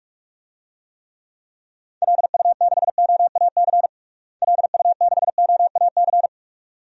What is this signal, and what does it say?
Morse code, heard as a keyed tone, about 42 wpm, 700 Hz. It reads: LU6OAC LU6OAC
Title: DXCC-100 Call Signs - 6 Characters - Repeated 2x - 45wpm